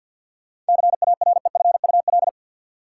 Morse code, heard as a keyed tone, about 52 wpm, 700 Hz. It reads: CAREFUL
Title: Sentences from Top 1600 Words - New Word Drill Alt - 50wpm